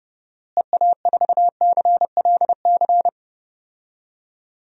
Morse code, heard as a keyed tone, about 30 wpm, 700 Hz. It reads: EA4CLC
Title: DXCC-100 Call Signs - 6 Characters - Repeated 2x - 30wpm